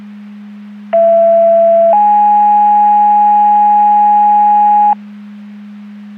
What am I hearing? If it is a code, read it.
Two-tone page, heard as a one-second tone, then a three-second tone, about 670 then 830 Hz. It is heard from 0.9 to 4.9 s.
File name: Portage 154.385 Paging